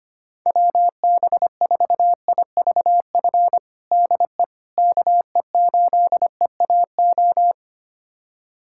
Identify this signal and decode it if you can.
Morse code, heard as a keyed tone, about 25 wpm, 700 Hz. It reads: WB4IVF DE KE8EAO
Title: USA Callsign DE Callsign 25wpm